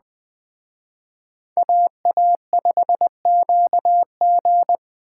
Morse code, heard as a keyed tone, about 19 wpm, 700 Hz. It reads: AA5QG